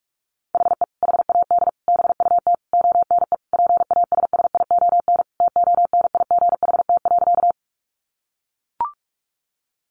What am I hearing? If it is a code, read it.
Morse code, heard as a keyed tone, about 44 wpm, 700 Hz. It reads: HE HAD BUT ONE PASSION TONIGHT.
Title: Sentences from Top 1300 Words 45wpm